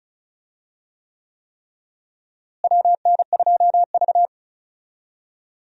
Morse code, heard as a keyed tone, about 34 wpm, 700 Hz. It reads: WN2V
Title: USA Call Signs - 4 Characters - Repeated 3x - 35wpm